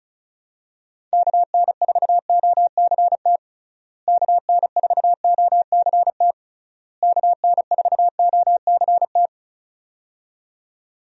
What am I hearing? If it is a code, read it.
Morse code, heard as a keyed tone, about 34 wpm, 700 Hz. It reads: KN4OCT KN4OCT KN4OCT
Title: USA Call Signs - 6 Characters - Repeated 3x - 35wpm